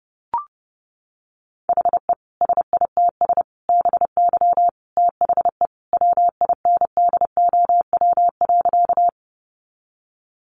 Morse code, heard as a keyed tone, about 30 wpm, 700 Hz. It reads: HE SITS BY THE WINDOW.
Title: Additional Easy Phrases 30wpm